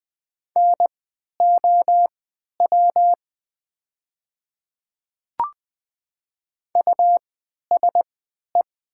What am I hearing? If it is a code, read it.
Morse code, heard as a keyed tone, about 20 wpm, spaced as for 11 wpm, 700 Hz. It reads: NOW USE
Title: 3x Character Spacing - Top 200 Words - 20wpm